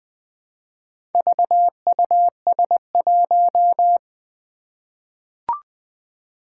Morse code, heard as a keyed tone, about 20 wpm, 700 Hz. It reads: VUS1
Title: Four Challenging Letter-Number Combinations 20wpm